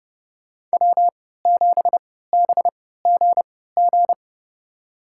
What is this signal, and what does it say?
Morse code, heard as a keyed tone, about 29 wpm, spaced as for 17 wpm, 700 Hz. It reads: W7BGG
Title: USA Call Signs - 3x Character Spacing - 30wpm